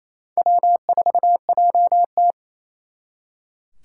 Morse code, heard as a keyed tone, about 37 wpm, 700 Hz. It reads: W4JT